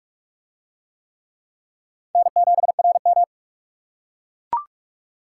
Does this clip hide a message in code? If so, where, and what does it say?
Yes: Morse code from 2.2 to 3.2 s, heard as a keyed tone, about 45 wpm, 700 Hz. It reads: N7RM